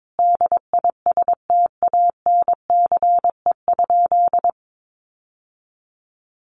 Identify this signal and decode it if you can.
Morse code, heard as a keyed tone, about 22 wpm, 700 Hz. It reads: DISTANCE?
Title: Sentences from Top 1200 Words 22wpm